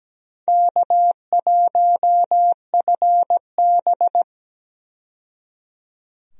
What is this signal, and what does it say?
Morse code, heard as a keyed tone, about 17 wpm, 700 Hz. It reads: K1FB
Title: USA Call Signs - 4 Characters - 17wpm